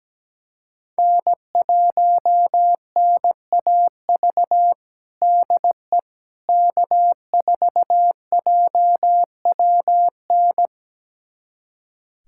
Morse code, heard as a keyed tone, about 17 wpm, 700 Hz. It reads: N1NAV DE K4JWN
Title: USA Callsign DE Callsign - 5 Characters - 17wpm